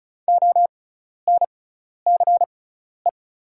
Morse code, heard as a keyed tone, about 36 wpm, spaced as for 12 wpm, 700 Hz. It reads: ONCE
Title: 6x Character Spacing - Top 500 Words - 35wpm